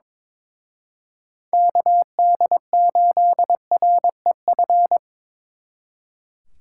Morse code, heard as a keyed tone, about 26 wpm, 700 Hz. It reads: KD8REF
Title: Speed Racing - USA 2x3 Call Signs 22wpm